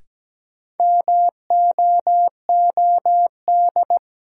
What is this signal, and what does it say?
Morse code, heard as a keyed tone, about 17 wpm, 700 Hz. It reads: MOOD
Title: Sentences from Top 1100 Words - New Word Drill - 17wpm